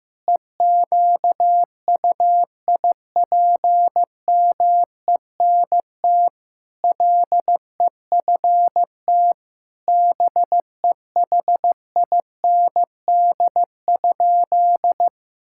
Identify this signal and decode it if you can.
Morse code, heard as a keyed tone, about 14 wpm, 700 Hz. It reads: EQUIPMENT LEFT BEHIND?
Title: Sentences from Top 1000 Words - Review Part 5 - 15wpm